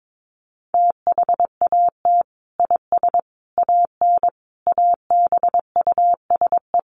Morse code, heard as a keyed tone, about 22 wpm, 700 Hz. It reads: THAT IS AN ABUSE
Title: Sentences from Top 1300 Words 22wpm